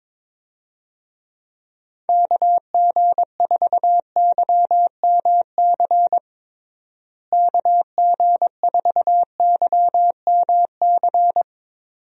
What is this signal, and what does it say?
Morse code, heard as a keyed tone, about 21 wpm, 700 Hz. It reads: KG4YMC KG4YMC
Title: USA Call Signs - 6 Characters - Repeated 2x - 22wpm